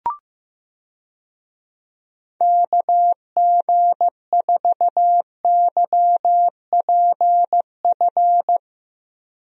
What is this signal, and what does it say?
Morse code, heard as a keyed tone, about 16 wpm, 700 Hz. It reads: KG4YPF
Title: USA Call Signs - 6 Characters - Repeated 3x - 15wpm